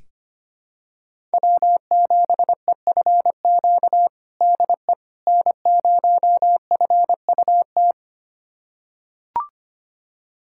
Morse code, heard as a keyed tone, about 25 wpm, 700 Hz. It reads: W7EFQ DE N0FUT
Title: USA Callsign DE Callsign - 5 Characters - 25wpm